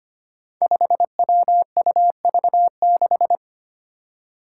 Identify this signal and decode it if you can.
Morse code, heard as a keyed tone, about 25 wpm, 700 Hz. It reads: 5WUV6
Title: Five Challenging Letter-Number Combinations 25wpm